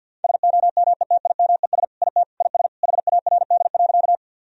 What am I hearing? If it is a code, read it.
Morse code, heard as a keyed tone, about 50 wpm, 700 Hz. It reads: SOMETIMES IT IS HARD.
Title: Sentences from Top 1000 Words - Review Part 3 - 50wpm